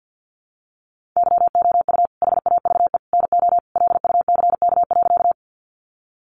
Morse code, heard as a keyed tone, about 50 wpm, 700 Hz. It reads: YOU HAVE NO LUCK.
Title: Sentences from Top 1500 Words 50wpm